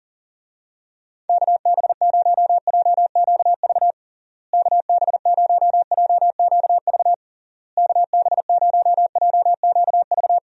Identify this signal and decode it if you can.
Morse code, heard as a keyed tone, about 37 wpm, 700 Hz. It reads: KB0JQV KB0JQV KB0JQV
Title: USA Call Signs - 6 Characters - Repeated 3x - 40wpm